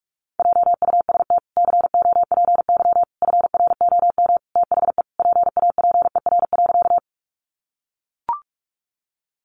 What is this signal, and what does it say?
Morse code, heard as a keyed tone, about 44 wpm, 700 Hz. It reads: JUST COPY FROM THE PAPER.
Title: Sentences from Top 1400 Words 45wpm